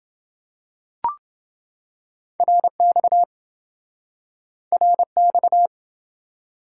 Morse code, heard as a keyed tone, about 27 wpm, 700 Hz. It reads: RX RX